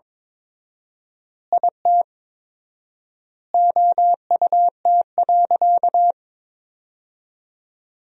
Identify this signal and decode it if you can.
Morse code, heard as a keyed tone, about 21 wpm, 700 Hz. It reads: IT OUT.
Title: 4x Word-Spacing - Sentences from Top 300 Words - 22wpm